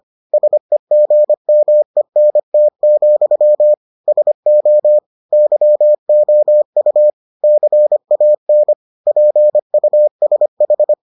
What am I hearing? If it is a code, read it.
Morse code, heard as a keyed tone, about 25 wpm, 600 Hz. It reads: SEGMENT, SO YOU CAN PUSH